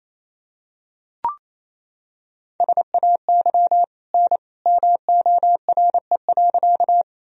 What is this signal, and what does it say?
Morse code, heard as a keyed tone, about 28 wpm, 700 Hz. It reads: SAY N MORE.
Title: Sentences from Top 200 Words and Abbreviations 28wpm